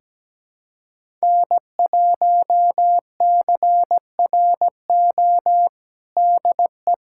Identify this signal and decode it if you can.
Morse code, heard as a keyed tone, about 17 wpm, 700 Hz. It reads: N1CRO DE
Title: USA Callsign DE Callsign - 5 Characters - 17wpm